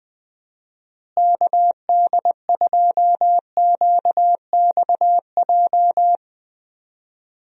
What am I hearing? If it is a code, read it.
Morse code, heard as a keyed tone, about 20 wpm, 700 Hz. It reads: KD2QXJ
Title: USA Call Signs - 6 Characters - Repeated 2x - 20wpm